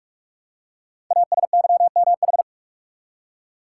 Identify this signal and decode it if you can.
Morse code, heard as a keyed tone, about 44 wpm, 700 Hz. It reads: ASYMH
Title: Code Groups - 5 Letters - 45wpm